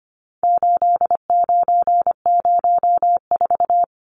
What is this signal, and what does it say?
Morse code, heard as a keyed tone, about 25 wpm, 700 Hz. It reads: 8904
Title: Numbers - 4-Digit 25wpm